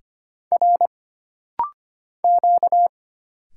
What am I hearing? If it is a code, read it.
Morse code, heard as a keyed tone, about 25 wpm, 700 Hz. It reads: R Q
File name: Single Character (Letter:Number:Punctuation:Prosign) - Rapid-Fire - 25wpm